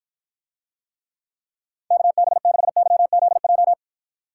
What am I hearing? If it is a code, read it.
Morse code, heard as a keyed tone, about 53 wpm, 700 Hz. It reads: KB6YZJ